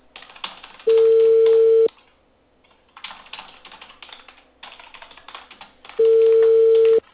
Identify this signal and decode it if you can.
Ringback tone (China, 450 Hz).